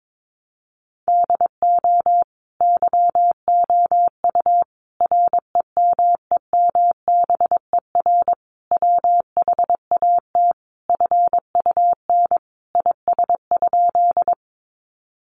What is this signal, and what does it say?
Morse code, heard as a keyed tone, about 22 wpm, 700 Hz. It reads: DO YOU REMEMBER WHAT FUN IS?
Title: Sentences from Top 700 Words 22wpm